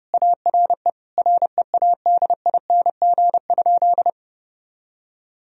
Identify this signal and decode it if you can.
Morse code, heard as a keyed tone, about 31 wpm, 700 Hz. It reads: ARE READING?
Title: Sentences from Top 1000 Words - Review Part 3 - 30wpm